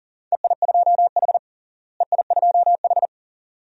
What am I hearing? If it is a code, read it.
Morse code, heard as a keyed tone, about 40 wpm, 700 Hz. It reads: EI2H EI2H
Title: DXCC-100 Call Signs - 4 Characters - Repeated 2x - 40wpm